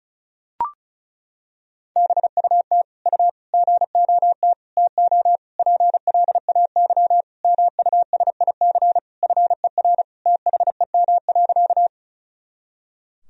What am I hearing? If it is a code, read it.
Morse code, heard as a keyed tone, about 36 wpm, 700 Hz. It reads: BUT U GOT TO PLAY MUSIC FER THEM.
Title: Sentences from Top 300 Words and Abbreviations 35wpm